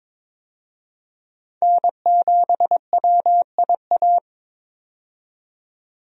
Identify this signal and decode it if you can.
Morse code, heard as a keyed tone, about 22 wpm, 700 Hz. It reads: N7WIA
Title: USA Call Signs - 5 Characters - 22wpm